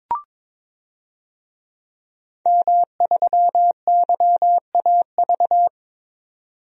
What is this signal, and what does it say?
Morse code, heard as a keyed tone, about 21 wpm, 700 Hz. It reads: M3YAV
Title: DXCC-100 Call Signs - 5 Characters - 22wpm